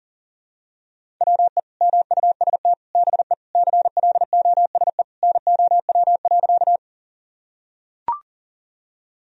Morse code, heard as a keyed tone, about 40 wpm, 700 Hz. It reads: WE MUST BE CLOSE NOW.
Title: Sentences from Top 1000 Words - Review Part 1 - 40wpm